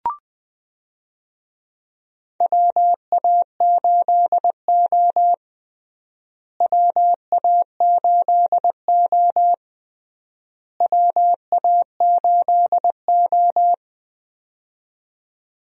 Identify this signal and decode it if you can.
Morse code, heard as a keyed tone, about 21 wpm, 700 Hz. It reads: WA8O WA8O WA8O
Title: USA Call Signs - 4 Characters - Repeated 3x - 20wpm